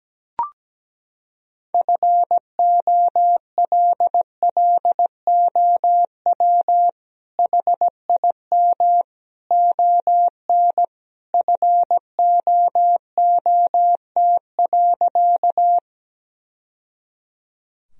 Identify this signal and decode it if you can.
Morse code, heard as a keyed tone, about 17 wpm, 700 Hz. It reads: FOLLOW HIM ON FOOT.
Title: Sentences from Top 1000 Words - Review Part 3 - 17wpm